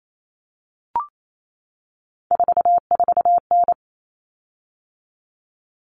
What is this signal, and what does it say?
Morse code, heard as a keyed tone, about 28 wpm, 700 Hz. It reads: 44N
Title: Common Cut Numbers - 3-Digit 28wpm